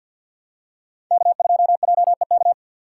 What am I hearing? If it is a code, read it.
Morse code, heard as a keyed tone, about 51 wpm, 700 Hz. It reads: K2JEK
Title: USA Call Signs - 5 Characters - 6x Character Spacing - 50wpm